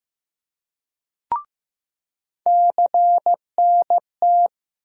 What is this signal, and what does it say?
Morse code, heard as a keyed tone, about 15 wpm, 700 Hz. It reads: CNT
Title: USA Call Sign Postfixes - 3 Letters 15wpm